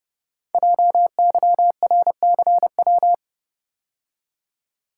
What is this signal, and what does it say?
Morse code, heard as a keyed tone, about 30 wpm, 700 Hz. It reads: JYRCW
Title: Code Groups - 5 Letter:Numbers - 30wpm